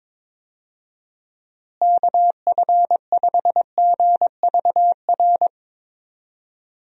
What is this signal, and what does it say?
Morse code, heard as a keyed tone, about 22 wpm, 700 Hz. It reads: KF5GVR
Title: USA Call Signs - 6 Characters - Repeated 3x - 22wpm